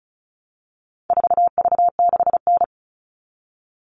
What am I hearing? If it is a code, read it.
Morse code, heard as a keyed tone, about 34 wpm, 700 Hz. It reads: 4V6N
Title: All Cut Numbers - 4-Digit 35wpm